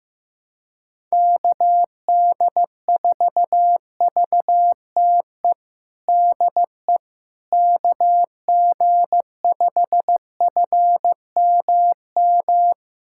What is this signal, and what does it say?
Morse code, heard as a keyed tone, about 15 wpm, 700 Hz. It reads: KD4VTE DE KG5FMM
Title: USA Callsign DE Callsign - 6 Characters - 15wpm